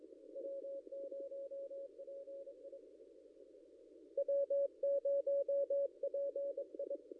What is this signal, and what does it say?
Morse code, heard as a keyed tone, about 22 wpm, 560 Hz. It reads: W0PI W0PI
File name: contests / ss-cw-k5tr-op-n5rz -0800